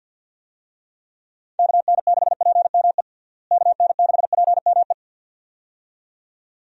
Morse code, heard as a keyed tone, about 52 wpm, 700 Hz. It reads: KN6PME KN6PME
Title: USA Call Signs - 6 Characters - Repeated 2x - 50wpm